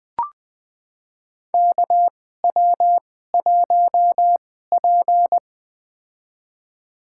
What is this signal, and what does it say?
Morse code, heard as a keyed tone, about 20 wpm, spaced as for 14 wpm, 700 Hz. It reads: KW1P
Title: USA Call Signs - 2x Character Spacing - 20wpm